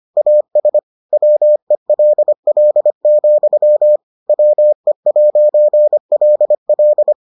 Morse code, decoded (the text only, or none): AS WELL, WE'LL